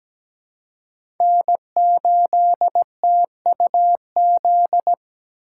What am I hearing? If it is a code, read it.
Morse code, heard as a keyed tone, about 17 wpm, 700 Hz. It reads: N8TUZ
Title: USA Call Signs - 5 Characters - 4x Character Spacing - 17wpm